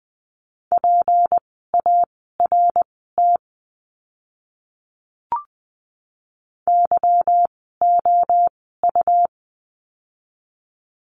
Morse code, heard as a keyed tone, about 20 wpm, spaced as for 15 wpm, 700 Hz. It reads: PART YOU
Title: 153 Lesson 25f - Letter p - Words 20wpm